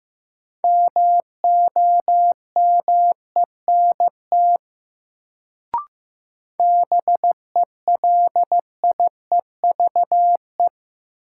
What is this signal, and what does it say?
Morse code, heard as a keyed tone, about 15 wpm, 700 Hz. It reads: MOMENT BELIEVE
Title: Sentences from Top 1000 Words - New Word Drill - 15wpm